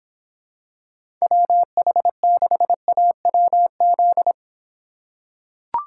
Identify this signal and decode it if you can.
Morse code, heard as a keyed tone, about 26 wpm, 700 Hz. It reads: WH6AWZ